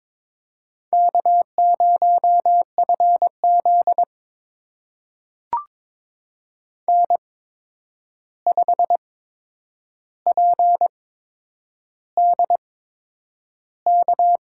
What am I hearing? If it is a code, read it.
Morse code, heard as a keyed tone, about 21 wpm, spaced as for 9 wpm, 700 Hz. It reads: K0FZ N5PDK
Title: USA Call Signs - 8x Character Spacing - 22wpm